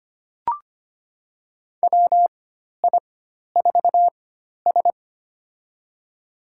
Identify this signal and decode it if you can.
Morse code, heard as a keyed tone, about 25 wpm, spaced as for 12 wpm, 700 Hz. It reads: WI4S
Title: USA Call Signs - 4 Characters - 4x Character Spacing - 25wpm